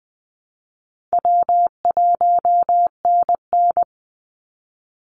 Morse code, heard as a keyed tone, about 20 wpm, 700 Hz. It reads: W1NN
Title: Call Signs Repeated 3x - 20wpm